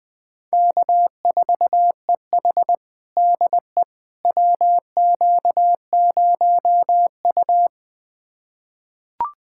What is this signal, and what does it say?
Morse code, heard as a keyed tone, about 20 wpm, 700 Hz. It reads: K4EH DE WQ0U
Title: USA Callsign DE Callsign - 4 Characters - 20wpm